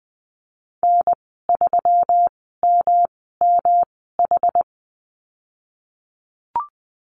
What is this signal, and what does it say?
Morse code, heard as a keyed tone, about 20 wpm, spaced as for 14 wpm, 700 Hz. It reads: N3MMH
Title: 140 Lesson 23g - Letter f - Call Signs 20wpm